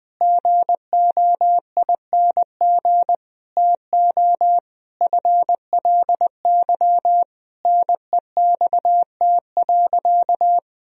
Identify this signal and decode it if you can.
Morse code, heard as a keyed tone, about 21 wpm, 700 Hz. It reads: GOING TO FLY NEXT.
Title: Sentences from Top 1000 Words - Review Part 2 - 20wpm